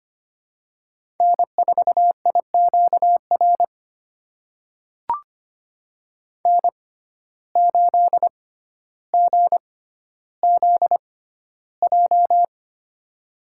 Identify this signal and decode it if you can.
Morse code, heard as a keyed tone, about 25 wpm, spaced as for 13 wpm, 700 Hz. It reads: N4IQR N8GZJ
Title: USA Call Signs - 5 Characters - 6x Character Spacing - 25wpm